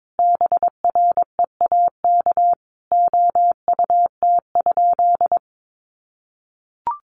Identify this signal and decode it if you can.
Morse code, heard as a keyed tone, about 22 wpm, 700 Hz. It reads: BREAK OUT?
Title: Sentences from Top 900 Words 22wpm